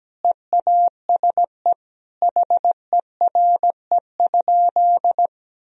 Morse code, heard as a keyed tone, about 17 wpm, 700 Hz. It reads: EASE HERE?
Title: Sentences from Top 1000 Words - Review Part 1 - 17wpm